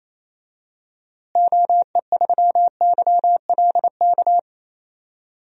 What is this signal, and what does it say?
Morse code, heard as a keyed tone, about 27 wpm, 700 Hz. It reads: OE3YLK
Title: DXCC-100 Call Signs - 6 Characters - Repeated 3x - 28wpm